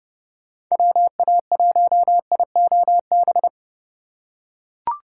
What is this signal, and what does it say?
Morse code, heard as a keyed tone, about 30 wpm, 700 Hz. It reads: WA1IOB
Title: USA Call Signs - 4x Character Spacing - 30wpm